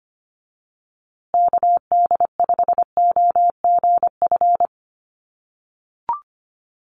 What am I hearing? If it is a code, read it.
Morse code, heard as a keyed tone, about 25 wpm, 700 Hz. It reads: KD5OGF
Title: USA Call Signs Repeated 2x 25wpm